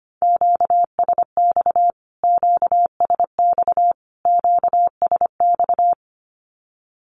Morse code, heard as a keyed tone, about 25 wpm, 700 Hz. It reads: QSX QSX QSX
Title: Q-Codes Repeated 3x 25wpm